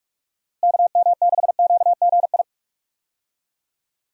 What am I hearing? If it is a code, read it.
Morse code, heard as a keyed tone, about 46 wpm, 700 Hz. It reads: KM6QGI
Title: USA Call Signs - 6 Characters - Repeated 2x - 45wpm